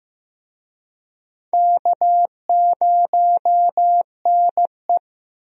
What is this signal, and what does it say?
Morse code, heard as a keyed tone, about 15 wpm, 700 Hz. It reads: K0NE
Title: USA Call Signs - 4 Characters - 15wpm